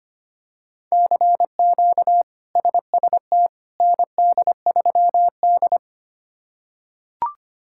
Morse code, heard as a keyed tone, about 25 wpm, 700 Hz. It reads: CQ SST ND3D
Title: K1USN SST QSO Practice - 25wpm